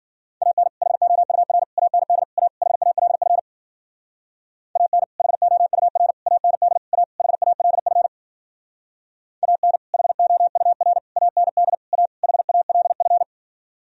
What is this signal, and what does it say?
Morse code, heard as a keyed tone, about 51 wpm, 700 Hz. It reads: AN HOUR AND A HALF AN HOUR AND A HALF AN HOUR AND A HALF